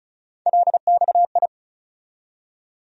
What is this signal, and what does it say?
Morse code, heard as a keyed tone, about 34 wpm, 700 Hz. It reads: LXI
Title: USA Call Sign Postfixes - 3 Letters 35wpm